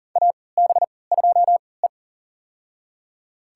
Morse code, heard as a keyed tone, about 37 wpm, spaced as for 22 wpm, 700 Hz. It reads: AB2E
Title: USA Call Signs - 4 Characters - 3x Character Spacing - 40wpm